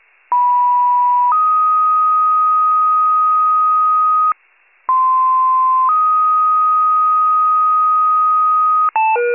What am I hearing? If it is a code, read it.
Two-tone page, heard as a one-second tone, then a three-second tone, about 980 then 1250 Hz, about 1040 then 1250 Hz. It is heard from 0.3 to 8.9 s.